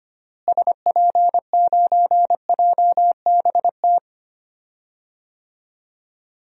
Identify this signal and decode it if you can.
Morse code, heard as a keyed tone, about 25 wpm, 700 Hz. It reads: SP9JBT
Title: DXCC-100 Call Signs - 6 Characters - Repeated 3x - 25wpm